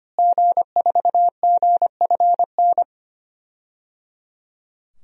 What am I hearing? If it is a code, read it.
Morse code, heard as a keyed tone, about 25 wpm, 700 Hz. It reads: G4GFN
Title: DXCC-100 Call Signs - 5 Characters - Repeated 3x - 25wpm